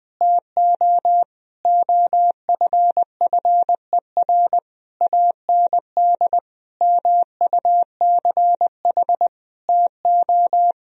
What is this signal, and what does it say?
Morse code, heard as a keyed tone, about 20 wpm, 700 Hz. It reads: TO OFFER AND MUCH TO
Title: Sentences from Top 1000 Words - Review Part 4 - 20wpm